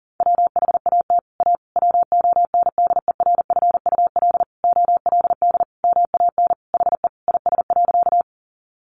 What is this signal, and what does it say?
Morse code, heard as a keyed tone, about 40 wpm, 700 Hz. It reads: WHAT A WONDERFUL OLD MAN HE IS.
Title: Sentences from Top 1100 Words 40wpm